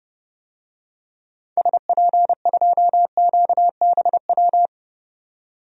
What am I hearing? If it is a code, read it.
Morse code, heard as a keyed tone, about 29 wpm, 700 Hz. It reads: SP2QBW
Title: DXCC-100 Call Signs - 6 Characters - 30wpm